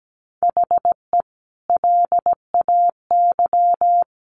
Morse code, heard as a keyed tone, about 17 wpm, 700 Hz. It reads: HE LAY
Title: A Sound Of Thunder 17wpm